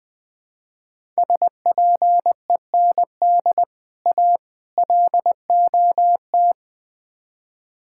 Morse code, heard as a keyed tone, about 20 wpm, 700 Hz. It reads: SPEND A LOT
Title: Build A Phrase - 5-Word Phrases - 20wpm